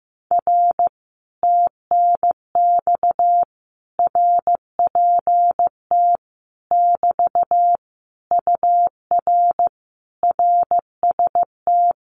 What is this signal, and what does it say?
Morse code, heard as a keyed tone, about 15 wpm, 700 Hz. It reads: R TNX RPT <BT> UR RST
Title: Common QSO Phrases Encore 15wpm